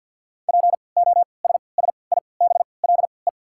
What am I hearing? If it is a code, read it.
Morse code, heard as a keyed tone, about 54 wpm, spaced as for 29 wpm, 690 Hz. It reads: POSSIBLE
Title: 3x Character Spacing - Top 500 Words - 50wpm